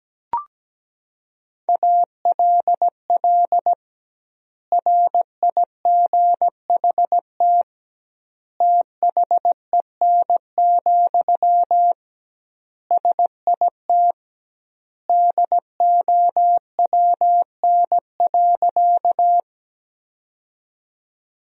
Morse code, heard as a keyed tone, about 17 wpm, 700 Hz. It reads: ALL RIGHT THEN, SIT DOWN.
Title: 2x Word-Spacing - Sentences from Top 500 Words - 17wpm